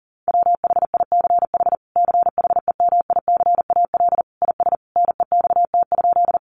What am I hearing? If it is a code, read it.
Morse code, heard as a keyed tone, about 40 wpm, 700 Hz. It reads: WHICH CHEMICAL IS NEXT?
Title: Sentences from Top 1200 Words 40wpm